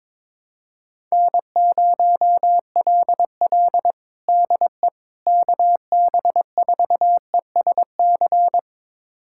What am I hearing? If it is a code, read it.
Morse code, heard as a keyed tone, about 22 wpm, 700 Hz. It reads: N0LL DE KB4ESC
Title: USA Callsign DE Callsign 22wpm